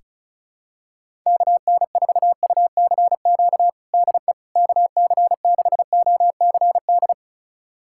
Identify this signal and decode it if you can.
Morse code, heard as a keyed tone, about 34 wpm, 700 Hz. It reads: KN4UCQ DE KC6OCD
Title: USA Callsign DE Callsign - 6 Characters - 35wpm